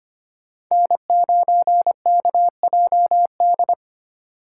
Morse code, heard as a keyed tone, about 25 wpm, 700 Hz. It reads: N9KJD
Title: USA Call Signs - 3x Character Spacing - 25wpm